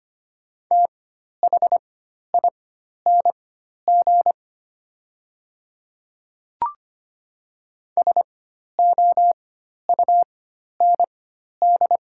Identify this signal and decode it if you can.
Morse code, heard as a keyed tone, about 25 wpm, spaced as for 12 wpm, 690 Hz. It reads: THING SOUND
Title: 4x Character Spacing - Top 200 Words - 25wpm